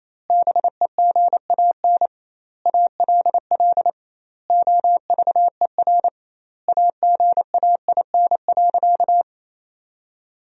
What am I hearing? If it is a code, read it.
Morse code, heard as a keyed tone, about 28 wpm, 700 Hz. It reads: BEGAN ALL OVER AGAIN.
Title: 2x Word-Spacing - Sentences from Top 500 Words - 28wpm